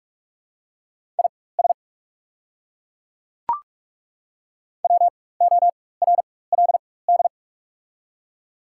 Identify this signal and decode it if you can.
Morse code, heard as a keyed tone, about 42 wpm, spaced as for 21 wpm, 690 Hz. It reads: IS WORLD
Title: 4x Character Spacing - Top 200 Words - 45wpm